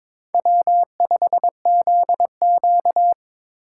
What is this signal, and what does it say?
Morse code, heard as a keyed tone, about 21 wpm, 700 Hz. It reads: W5ZQ
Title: USA Call Signs - 4 Characters - 6x Character Spacing - 22wpm